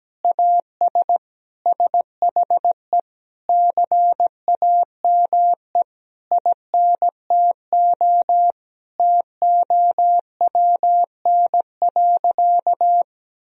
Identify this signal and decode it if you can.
Morse code, heard as a keyed tone, about 17 wpm, 700 Hz. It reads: AS SHE CAME INTO TOWN.
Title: Sentences from Top 1000 Words - Review Part 2 - 17wpm